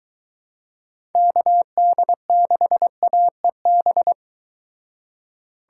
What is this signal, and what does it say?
Morse code, heard as a keyed tone, about 23 wpm, 700 Hz. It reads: KD6AEB